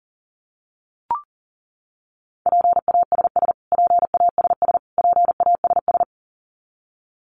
Morse code, heard as a keyed tone, about 40 wpm, 700 Hz. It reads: PASS PASS PASS
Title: Top 500 Words Repeated 40wpm